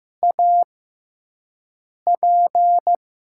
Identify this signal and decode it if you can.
Morse code, heard as a keyed tone, about 15 wpm, spaced as for 5 wpm, 700 Hz. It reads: AP
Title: USA Call Signs - 5 Characters - 6x Character Spacing - 15wpm